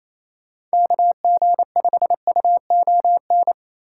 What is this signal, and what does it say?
Morse code, heard as a keyed tone, about 37 wpm, 700 Hz. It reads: KG5UON